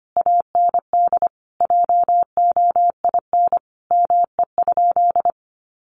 Morse code, heard as a keyed tone, about 25 wpm, 700 Hz. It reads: AND JOIN ME?
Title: Sentences from Top 1400 Words 25wpm